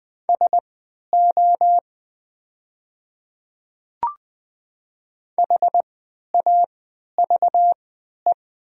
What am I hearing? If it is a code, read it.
Morse code, heard as a keyed tone, about 20 wpm, spaced as for 11 wpm, 700 Hz. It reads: SO HAVE
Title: 3x Character Spacing - Top 100 Words - 20wpm